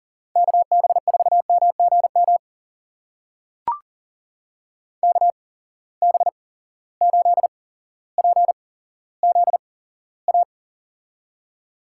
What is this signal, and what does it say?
Morse code, heard as a keyed tone, about 40 wpm, spaced as for 17 wpm, 700 Hz. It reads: KB4MGM KB8PZA
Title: USA Call Signs - 6 Characters - 8x Character Spacing - 40wpm